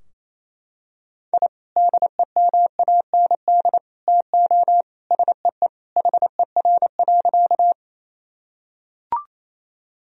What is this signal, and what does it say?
Morse code, heard as a keyed tone, about 27 wpm, 700 Hz. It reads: I DEMAND TO SEE HER.
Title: Sentences from Top 1000 Words - Review Part 4 - 28wpm